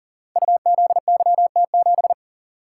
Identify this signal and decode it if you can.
Morse code, heard as a keyed tone, about 39 wpm, 700 Hz. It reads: UZYT7
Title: Code Groups - 5 Letter:Numbers - 40wpm